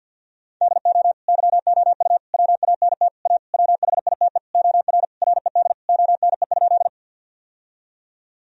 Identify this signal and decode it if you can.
Morse code, heard as a keyed tone, about 50 wpm, 700 Hz. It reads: DO YOU WANT A WHITE OR RED ONE?
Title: Sentences from Top 1000 Words - Review Part 2 - 50wpm